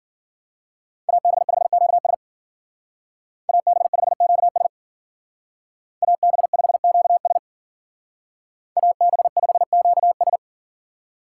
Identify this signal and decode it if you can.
Morse code, heard as a keyed tone, about 48 wpm, 700 Hz. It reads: AB5QS AB5QS AB5QS AB5QS
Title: Speed Racing - USA 2x2 Call Signs 40wpm